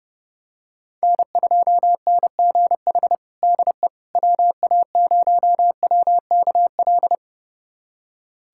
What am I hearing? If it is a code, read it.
Morse code, heard as a keyed tone, about 30 wpm, 700 Hz. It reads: N2NGH DE WA0WKL
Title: USA Callsign DE Callsign 30wpm